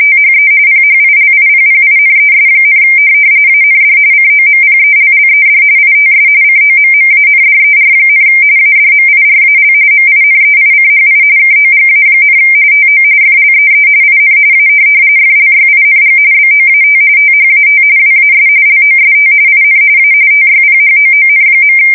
RTTY, about 45.45 baud. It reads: THE PHILIPPINES, AND COMPELLED ITS COMMANDER, ADMIRAL MINEICHI KOGA, TO FLEE IN A FLYING BOAT THAT WAS LOST IN A TYPHOON.
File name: fsk170 / USS-New-Jersey